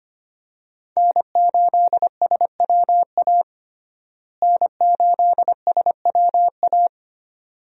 Morse code, heard as a keyed tone, about 25 wpm, 700 Hz. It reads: N8SWA N8SWA
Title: USA Call Signs - 5 Characters - Repeated 2x - 25wpm